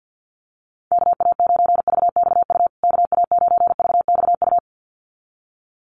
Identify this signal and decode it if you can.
Morse code, heard as a keyed tone, about 50 wpm, 700 Hz. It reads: KA9VXU KA9VXU
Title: USA Call Signs Repeated 2x 50wpm